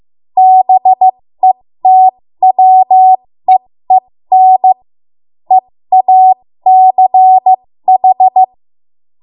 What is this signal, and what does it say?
Morse code, heard as a keyed tone, about 15 wpm, spaced as for 13 wpm, 750 Hz. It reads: BETWEEN EACH
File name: Morse / 13WPM